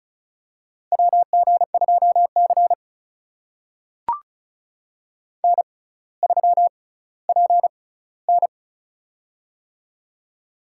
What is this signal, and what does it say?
Morse code, heard as a keyed tone, about 34 wpm, spaced as for 18 wpm, 700 Hz. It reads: WG2C N3PN
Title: USA Call Signs - 4 Characters - 6x Character Spacing - 35wpm